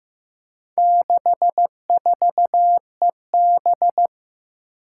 Morse code, heard as a keyed tone, about 15 wpm, 700 Hz. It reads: 64EB
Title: Four Challenging Letter-Number Combinations 15wpm